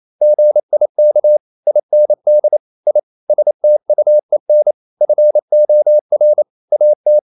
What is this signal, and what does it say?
Morse code, heard as a keyed tone, about 28 wpm, 600 Hz. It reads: GIK IND I STUEN FOR AT
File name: Ferie QRQ 28wpm